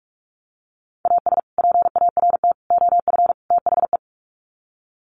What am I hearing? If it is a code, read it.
Morse code, heard as a keyed tone, about 44 wpm, 700 Hz. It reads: AS PART OF THE
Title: Sets of 4 Words 45wpm